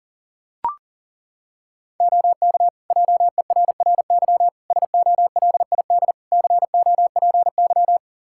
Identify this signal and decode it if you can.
Morse code, heard as a keyed tone, about 37 wpm, 700 Hz. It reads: OK JERRY SOLID COPY
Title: Practice QSO - Exchange 3 - More Info - 40wpm